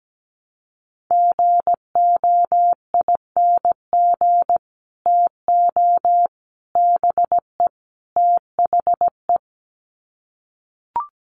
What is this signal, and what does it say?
Morse code, heard as a keyed tone, about 17 wpm, 700 Hz. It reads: GOING TO BE THE
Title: Sets of 4 Words 17wpm